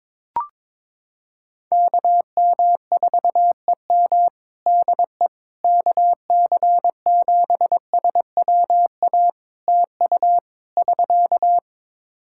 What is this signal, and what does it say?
Morse code, heard as a keyed tone, about 22 wpm, 700 Hz. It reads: KM4EM DE KC7SWA TU <SK>
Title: Practice QSO - Exchange 4 - Ending - 22wpm